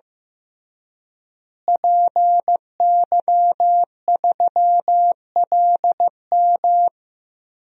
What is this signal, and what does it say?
Morse code, heard as a keyed tone, about 15 wpm, 700 Hz. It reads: PY3LM
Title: DXCC-100 Call Signs - 5 Characters - Repeated 3x - 15wpm